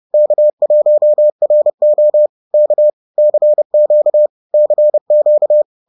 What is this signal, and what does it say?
Morse code, heard as a keyed tone, about 30 wpm, 600 Hz. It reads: K1RO K CQ CQ